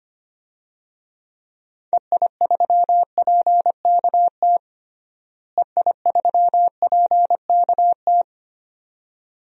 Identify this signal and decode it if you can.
Morse code, heard as a keyed tone, about 25 wpm, 700 Hz. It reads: EI3PKT EI3PKT
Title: DXCC-100 Call Signs - 6 Characters - Repeated 2x - 25wpm